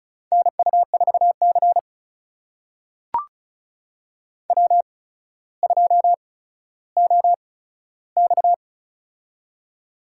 Morse code, heard as a keyed tone, about 36 wpm, spaced as for 15 wpm, 700 Hz. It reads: NU4C W2OX
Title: USA Call Signs - 4 Characters - 8x Character Spacing - 35wpm